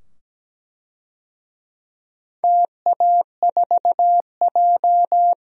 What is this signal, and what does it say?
Morse code, heard as a keyed tone, about 17 wpm, 700 Hz. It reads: TA4J
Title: DXCC-100 Call Signs - 4 Characters - 17wpm